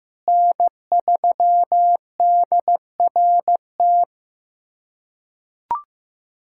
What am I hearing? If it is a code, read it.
Morse code, heard as a keyed tone, about 15 wpm, 700 Hz. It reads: N3DRT
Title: USA Callsign DE Callsign - 5 Characters - 15wpm